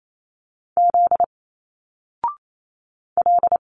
Morse code, heard as a keyed tone, about 28 wpm, 700 Hz. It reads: Z L
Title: Single Letters 28wpm